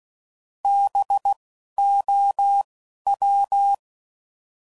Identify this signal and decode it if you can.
Morse code, heard as a keyed tone, about 16 wpm, 800 Hz. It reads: B O W